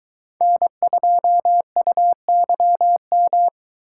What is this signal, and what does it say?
Morse code, heard as a keyed tone, about 23 wpm, 700 Hz. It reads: N2UYM